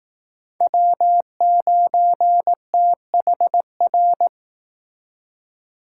Morse code, heard as a keyed tone, about 18 wpm, 700 Hz. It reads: W9THR
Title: Speed Racing - USA 1x3 Call Signs 15wpm